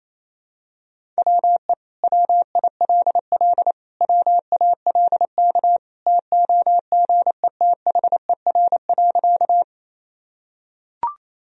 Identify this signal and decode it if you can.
Morse code, heard as a keyed tone, about 27 wpm, 700 Hz. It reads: WE WILL WALK TOGETHER.
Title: Sentences from Top 1000 Words - Review Part 1 - 28wpm